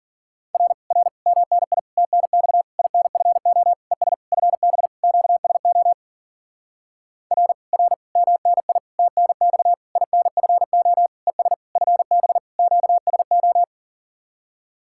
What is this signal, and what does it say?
Morse code, heard as a keyed tone, about 52 wpm, 700 Hz. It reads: R R MNI TNX INFO ES FB QSO R R MNI TNX INFO ES FB QSO